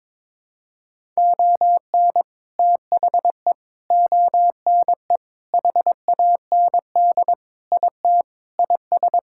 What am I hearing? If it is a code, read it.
Morse code, heard as a keyed tone, about 22 wpm, 700 Hz. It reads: ON THE ONE HAND IT IS
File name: practice / 22wpm / Sentences with 8-Word Phrases - 22wpm